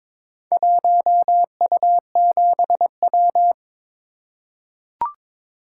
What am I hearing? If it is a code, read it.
Morse code, heard as a keyed tone, about 23 wpm, 690 Hz. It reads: 1U7W
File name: practice / 22wpm / Four Letter-Number Combinations - 22wpm